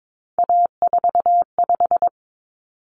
Morse code, heard as a keyed tone, about 22 wpm, 700 Hz. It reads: A45
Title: Common Cut Numbers - 3-Digit 22wpm